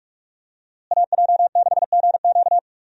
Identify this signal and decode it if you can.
Morse code, heard as a keyed tone, about 42 wpm, 700 Hz. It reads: AJ6GQ
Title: USA Call Signs - 5 Characters - Repeated 3x - 45wpm